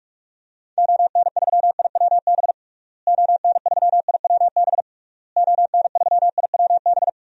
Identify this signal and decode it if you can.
Morse code, heard as a keyed tone, about 41 wpm, 700 Hz. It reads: ON3IWB ON3IWB ON3IWB
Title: DXCC-100 Call Signs - 6 Characters - Repeated 3x - 45wpm